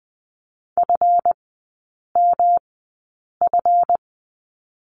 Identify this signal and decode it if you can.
Morse code, heard as a keyed tone, about 20 wpm, spaced as for 8 wpm, 700 Hz. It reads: FMF
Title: 138 Lesson 23e - Letter f - Three Characters 20wpm